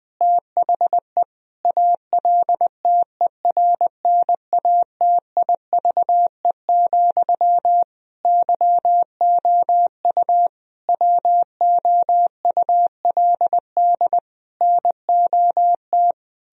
Morse code, heard as a keyed tone, about 21 wpm, 700 Hz. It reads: THE ALTERNATIVE, YOU WOULD NOT
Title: Sentences from Top 1000 Words - Review Part 4 - 20wpm